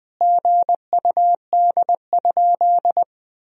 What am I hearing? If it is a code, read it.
Morse code, heard as a keyed tone, about 20 wpm, 700 Hz. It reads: GUD?
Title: Sentences from Top 200 Words and Abbreviations 20wpm